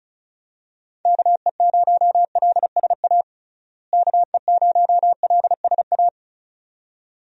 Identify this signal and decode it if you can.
Morse code, heard as a keyed tone, about 36 wpm, 700 Hz. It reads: KE0LSA KE0LSA
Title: USA Call Signs - 6 Characters - Repeated 2x - 35wpm